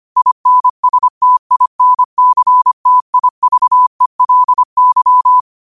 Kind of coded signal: Morse code